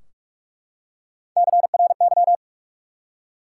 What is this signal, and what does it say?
Morse code, heard as a keyed tone, about 44 wpm, 700 Hz. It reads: CRY
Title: Three Letter Words - 45wpm